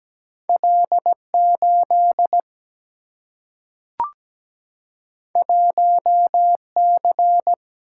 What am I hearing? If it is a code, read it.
Morse code, heard as a keyed tone, about 17 wpm, 700 Hz. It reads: L8 1C
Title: Single-Letter Single-Number 17wpm